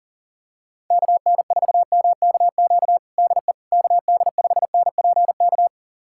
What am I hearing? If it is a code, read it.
Morse code, heard as a keyed tone, about 40 wpm, 700 Hz. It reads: KN4MKQ DE KD5NPK
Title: USA Callsign DE Callsign - 6 Characters - 40wpm